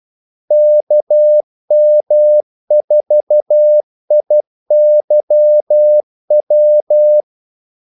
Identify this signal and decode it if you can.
Morse code, heard as a keyed tone, about 12 wpm, 600 Hz. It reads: KM4IYW
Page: 2 ekstra ordmellemrum mellem kaldesignalerne